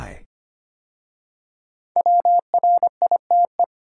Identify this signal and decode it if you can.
Morse code, heard as a keyed tone, about 25 wpm, 700 Hz. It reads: WRITE
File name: 5 Letter Words - First 3 Letters Given - 25wpm